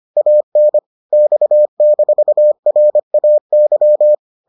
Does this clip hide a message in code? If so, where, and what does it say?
Yes: Morse code from 0.2 to 4.1 s, heard as a keyed tone, about 25 wpm, 600 Hz. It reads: AN X-RAY